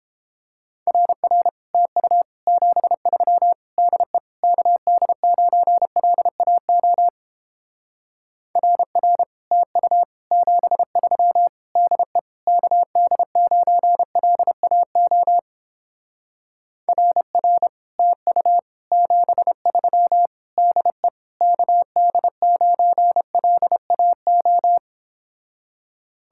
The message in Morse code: RR TU 73 DE KD9LAO RR TU 73 DE KD9LAO RR TU 73 DE KD9LAO

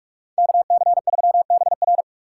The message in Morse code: KC3BR

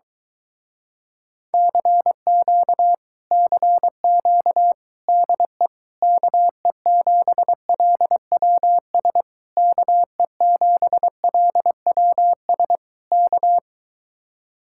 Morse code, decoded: CQ CQ DE KE7LWS KE7LWS K

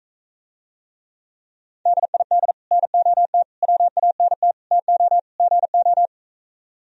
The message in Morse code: DID NOT WANT TO GO

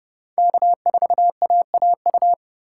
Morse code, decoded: K4AAU

